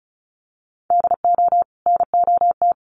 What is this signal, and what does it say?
Morse code, heard as a keyed tone, about 34 wpm, 700 Hz. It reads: DO NOT
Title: Sets of 2 Words 35wpm